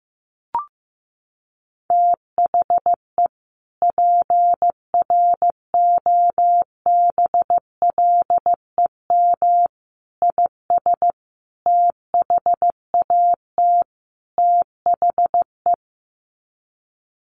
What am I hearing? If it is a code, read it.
Morse code, heard as a keyed tone, about 15 wpm, 700 Hz. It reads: THE PROBLEM IS THAT THE
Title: Sets of 5 Words Encore2 15wpm